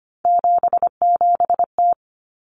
Morse code, heard as a keyed tone, about 25 wpm, 700 Hz. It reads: 77T
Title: Common Cut Numbers - 3-Digit 25wpm